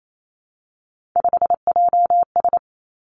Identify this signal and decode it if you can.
Morse code, heard as a keyed tone, about 28 wpm, 700 Hz. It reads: 5JS
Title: Three Challenging Letter-Number Combinations 28wpm